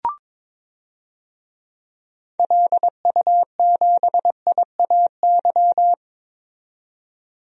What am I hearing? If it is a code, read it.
Morse code, heard as a keyed tone, about 22 wpm, 700 Hz. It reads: LU7IAY
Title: DXCC-100 Call Signs - 6 Characters - 22wpm